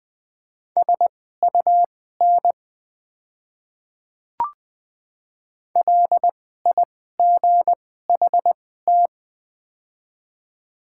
Morse code, heard as a keyed tone, about 20 wpm, spaced as for 14 wpm, 700 Hz. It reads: SUN LIGHT
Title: 2x Character Spacing - Top 500 Words - 20wpm